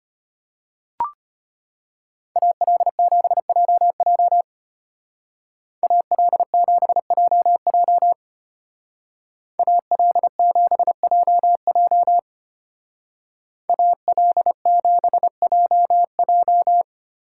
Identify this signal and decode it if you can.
Morse code, heard as a keyed tone, about 33 wpm, 700 Hz. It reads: AL7JJ AL7JJ AL7JJ AL7JJ